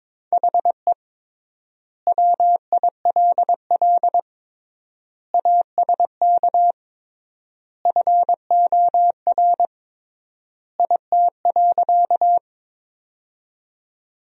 Morse code, heard as a keyed tone, about 22 wpm, 700 Hz. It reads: HE WILL ASK FOR IT.
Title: 3x Word-Spacing - Sentences from Top 200 Words - 22wpm